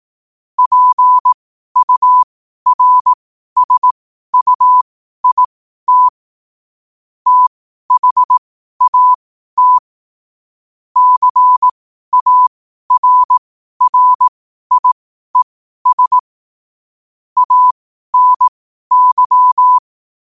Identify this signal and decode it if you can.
Morse code, heard as a keyed tone, about 18 wpm, spaced as for 12 wpm, 1000 Hz. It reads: PURSUIT THAT CARRIES ANY
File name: qotdfiles / 12wpm